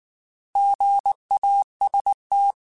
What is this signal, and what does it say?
Morse code, heard as a keyed tone, about 19 wpm, 800 Hz. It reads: GAST